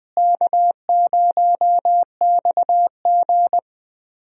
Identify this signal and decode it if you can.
Morse code, heard as a keyed tone, about 21 wpm, 700 Hz. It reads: K0XG
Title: USA Call Signs - 4 Characters - Repeated 3x - 20wpm